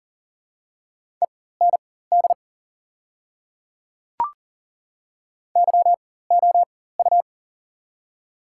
Morse code, heard as a keyed tone, about 40 wpm, spaced as for 19 wpm, 700 Hz. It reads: END YOU
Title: 4x Character Spacing - Top 300 Words - 40wpm